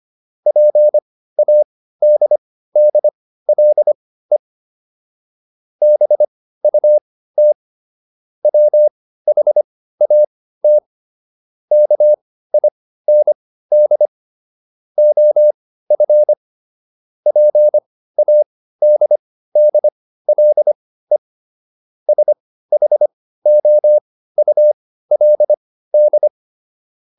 Morse code, heard as a keyed tone, about 25 wpm, spaced as for 15 wpm, 600 Hz. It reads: PADDLE BUT WHAT KIND OF PADDLE SHOULD